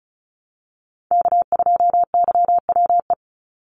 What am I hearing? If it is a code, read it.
Morse code, heard as a keyed tone, about 36 wpm, 700 Hz. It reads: K2YWE
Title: Call Signs - 35wpm